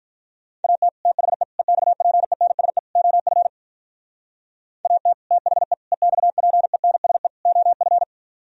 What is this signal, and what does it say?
Morse code, heard as a keyed tone, about 46 wpm, 700 Hz. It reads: AT THE EXPENSE OF AT THE EXPENSE OF